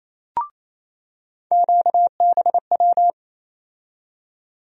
Morse code, heard as a keyed tone, about 27 wpm, 700 Hz. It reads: QBW
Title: USA Call Sign Postfixes - 3 Letters 28wpm